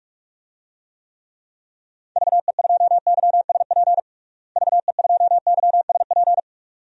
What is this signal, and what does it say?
Morse code, heard as a keyed tone, about 46 wpm, 700 Hz. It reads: VE2YSP VE2YSP
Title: DXCC-100 Call Signs - 6 Characters - Repeated 2x - 45wpm